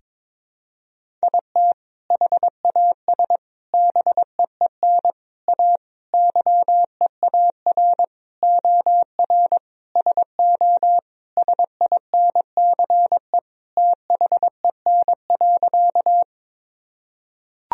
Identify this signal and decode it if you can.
Morse code, heard as a keyed tone, about 22 wpm, 700 Hz. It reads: IT HAS BEEN A YEAR OR SO SINCE THEN.
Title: Sentences from Top 1000 Words - Review Part 1 - 22wpm